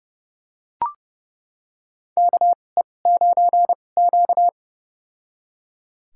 Morse code, heard as a keyed tone, about 30 wpm, spaced as for 22 wpm, 700 Hz. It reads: KE9Q
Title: USA Call Signs - 4 Characters - 2x Character Spacing - 30wpm